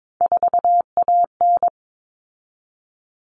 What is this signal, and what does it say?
Morse code, heard as a keyed tone, about 22 wpm, 700 Hz. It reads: 4AN
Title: Common Cut Numbers - 3-Digit 22wpm